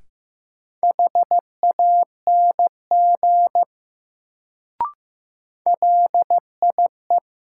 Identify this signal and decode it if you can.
Morse code, heard as a keyed tone, about 15 wpm, 700 Hz. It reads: HANG LIE
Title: Sentences from Top 1600 Words - New Word Drill Alt - 15wpm